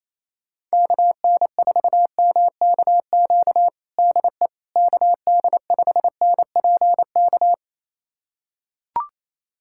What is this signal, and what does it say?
Morse code, heard as a keyed tone, about 27 wpm, 700 Hz. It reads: KN4MKQ DE KD5NPK
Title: USA Callsign DE Callsign - 6 Characters - 28wpm